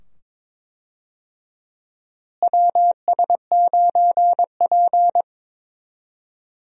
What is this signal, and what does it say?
Morse code, heard as a keyed tone, about 21 wpm, 700 Hz. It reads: WS9P
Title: USA Call Signs - 4 Characters - Repeated 2x - 22wpm